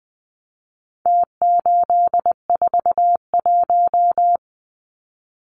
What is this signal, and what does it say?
Morse code, heard as a keyed tone, about 19 wpm, 700 Hz. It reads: T841
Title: Numbers - Contest Style - 4-Digit 20wpm